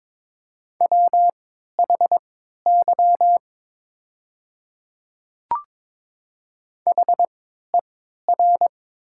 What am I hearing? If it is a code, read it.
Morse code, heard as a keyed tone, about 22 wpm, spaced as for 13 wpm, 700 Hz. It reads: WHY HER
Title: 3x Character Spacing - Top 200 Words - 22wpm